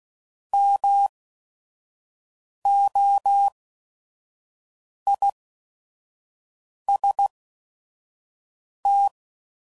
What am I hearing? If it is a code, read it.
Morse code, heard as a keyed tone, about 16 wpm, spaced as for 5 wpm, 800 Hz. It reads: MOIST